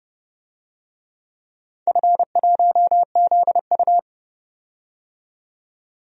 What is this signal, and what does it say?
Morse code, heard as a keyed tone, about 30 wpm, 700 Hz. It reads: F1ZU
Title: DXCC-100 Call Signs - 4 Characters - 30wpm